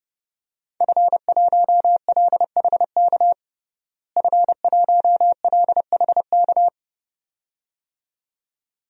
Morse code, heard as a keyed tone, about 30 wpm, 700 Hz. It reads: F1LHK F1LHK
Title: DXCC-100 Call Signs - 5 Characters - Repeated 2x - 30wpm